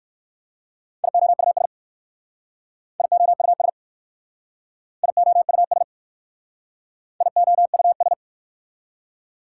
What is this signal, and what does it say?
Morse code, heard as a keyed tone, about 54 wpm, 700 Hz. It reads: IOUS IOUS IOUS IOUS
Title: Speed-Racing Prefixes and Suffixes 45wpm